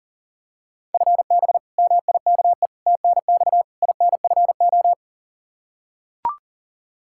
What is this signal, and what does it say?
Morse code, heard as a keyed tone, about 40 wpm, 700 Hz. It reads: FB MIKE TNX INFO
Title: Practice QSO - Exchange 3 - More Info - 40wpm